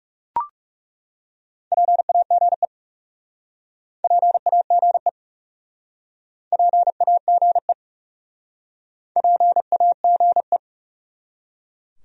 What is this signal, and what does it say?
Morse code, heard as a keyed tone, about 42 wpm, 700 Hz. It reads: PAGE PAGE PAGE PAGE